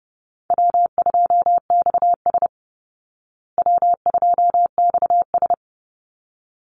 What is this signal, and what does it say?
Morse code, heard as a keyed tone, about 30 wpm, 700 Hz. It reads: W2XS W2XS
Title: Call Signs Repeated 3x - 30wpm